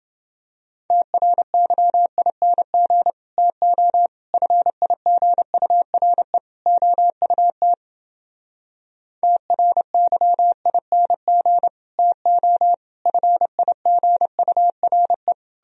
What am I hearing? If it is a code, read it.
Morse code, heard as a keyed tone, about 31 wpm, 700 Hz. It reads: TRYING TO FIGURE OUT TRYING TO FIGURE